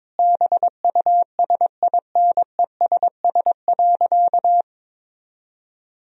Morse code, heard as a keyed tone, about 22 wpm, 700 Hz. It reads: BUSINESS.
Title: Sentences from Top 1000 Words - Review Part 3 - 22wpm